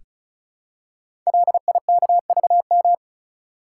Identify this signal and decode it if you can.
Morse code, heard as a keyed tone, about 36 wpm, 700 Hz. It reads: LIKVM
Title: Code Groups - 5 Letters - 35wpm